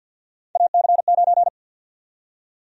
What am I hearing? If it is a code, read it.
Morse code, heard as a keyed tone, about 53 wpm, 700 Hz. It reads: AC9
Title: USA Call Sign Prefixes - 50wpm